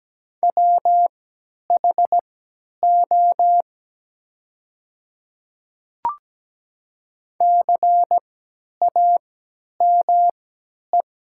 Morse code, heard as a keyed tone, about 17 wpm, spaced as for 10 wpm, 700 Hz. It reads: WHO CAME
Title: 3x Character Spacing - Top 200 Words - 17wpm